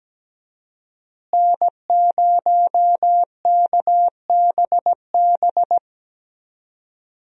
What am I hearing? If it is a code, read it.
Morse code, heard as a keyed tone, about 17 wpm, 700 Hz. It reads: N0KBB
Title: USA Call Signs - 5 Characters - Repeated 2x - 17wpm